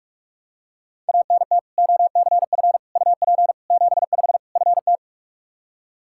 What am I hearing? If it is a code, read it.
Morse code, heard as a keyed tone, about 44 wpm, 700 Hz. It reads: ANT OCF UP 75 FT
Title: Practice QSO - Exchange 2 - Info - 45wpm